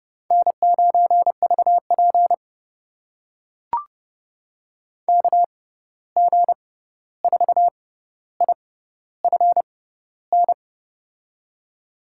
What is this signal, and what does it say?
Morse code, heard as a keyed tone, about 30 wpm, spaced as for 14 wpm, 700 Hz. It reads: N9VP KG4IFN
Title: USA Call Signs - 6x Character Spacing - 30wpm